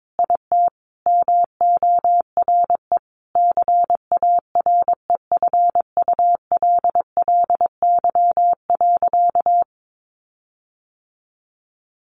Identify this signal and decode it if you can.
Morse code, heard as a keyed tone, about 22 wpm, 700 Hz. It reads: IT MORE CAREFULLY.
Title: Sentences from Top 1300 Words 22wpm